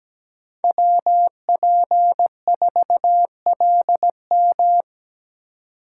Morse code, heard as a keyed tone, about 17 wpm, 700 Hz. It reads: WP4LM
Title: USA Call Signs - 3x Character Spacing - 17wpm